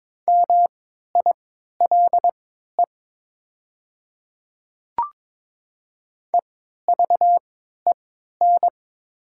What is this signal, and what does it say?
Morse code, heard as a keyed tone, about 22 wpm, spaced as for 13 wpm, 700 Hz. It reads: MILE EVEN
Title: 3x Character Spacing - Top 300 Words - 22wpm